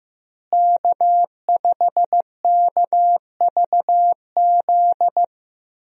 Morse code, heard as a keyed tone, about 15 wpm, 700 Hz. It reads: K5KVZ
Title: USA Call Signs - 5 Characters - Repeated 3x - 15wpm